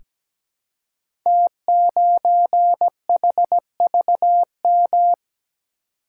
Morse code, heard as a keyed tone, about 17 wpm, 700 Hz. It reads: T9HVM
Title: Code Groups - 5 Letter:Numbers - 17wpm